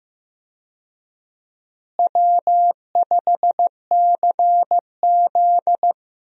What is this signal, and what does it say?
Morse code, heard as a keyed tone, about 15 wpm, 700 Hz. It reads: W5CZ
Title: USA Call Signs - 4 Characters - Repeated 2x - 15wpm